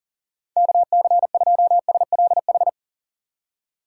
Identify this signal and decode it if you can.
Morse code, heard as a keyed tone, about 37 wpm, 700 Hz. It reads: KC2SLH
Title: USA Call Signs - 6 Characters - 4x Character Spacing - 40wpm